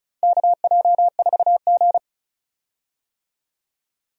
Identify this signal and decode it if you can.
Morse code, heard as a keyed tone, about 36 wpm, 700 Hz. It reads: KJ4G
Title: USA Call Signs - 4 Characters - 35wpm